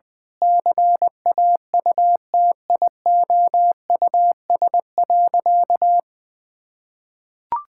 Morse code, heard as a keyed tone, about 20 wpm, 700 Hz. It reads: CAUTIOUS.
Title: Sentences with 8-Word Phrases - Spoken Beginning - 20wpm